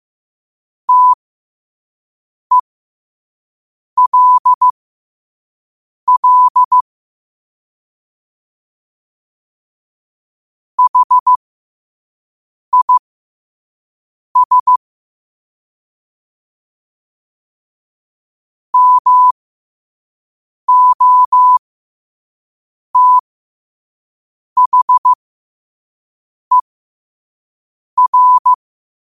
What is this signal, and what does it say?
Morse code, heard as a keyed tone, about 15 wpm, spaced as for 5 wpm, 1000 Hz. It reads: TELL HIS MOTHER
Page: Quotes for Tue, 12 Aug 2025 in Morse Code at 5 words per minute.